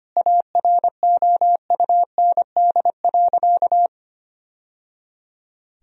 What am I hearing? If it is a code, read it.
Morse code, heard as a keyed tone, about 25 wpm, 700 Hz. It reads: AROUND.
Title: Sentences from Top 1000 Words - Review Part 3 - 25wpm